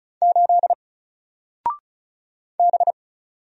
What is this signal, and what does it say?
Morse code, heard as a keyed tone, about 34 wpm, 700 Hz. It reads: 8 B
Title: Single Character (Letter:Number:Punctuation:Prosign) 35wpm